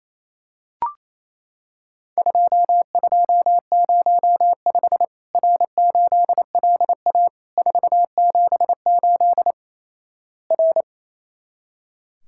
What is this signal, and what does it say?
Morse code, heard as a keyed tone, about 28 wpm, 700 Hz. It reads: 2205 R8LA 478 R
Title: Worked All Europe DX Contest - QTC Exchange - 28wpm